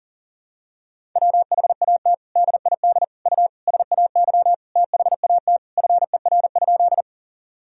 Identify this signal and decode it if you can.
Morse code, heard as a keyed tone, about 40 wpm, 700 Hz. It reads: WHAT DID U SAY THAT FER?
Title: Sentences from Top 200 Words and Abbreviations 40wpm